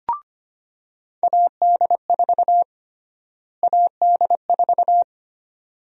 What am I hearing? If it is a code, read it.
Morse code, heard as a keyed tone, about 25 wpm, 700 Hz. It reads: AD4 AD4
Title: USA Call Sign Prefixes - Repeated 2x - 25wpm